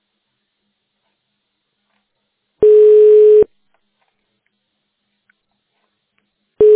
Ringback tone (425 Hz, 1 s on, 3 s off).